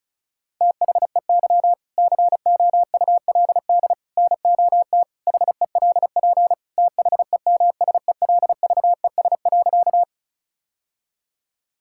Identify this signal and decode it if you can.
Morse code, heard as a keyed tone, about 36 wpm, 700 Hz. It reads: THEY COULD NOT HELP THEMSELVES.
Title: Sentences from Top 1000 Words - Review Part 5 - 35wpm